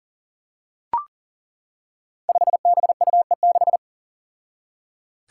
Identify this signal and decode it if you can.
Morse code, heard as a keyed tone, about 39 wpm, 700 Hz. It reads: 5BUE6
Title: Five Challenging Letter-Number Combinations 40wpm